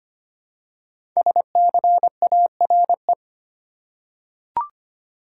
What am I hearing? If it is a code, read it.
Morse code, heard as a keyed tone, about 25 wpm, 700 Hz. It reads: SCARE
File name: 5 Letter Words - First 3 Letters Given - 25wpm